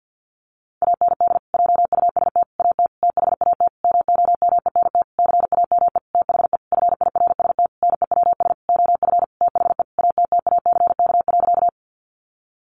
Morse code, heard as a keyed tone, about 50 wpm, 700 Hz. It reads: AND JUST AT THAT MOMENT CAME THE FIRST NEWS OF THE ATTACK.
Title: Sentences from Top 1500 Words 50wpm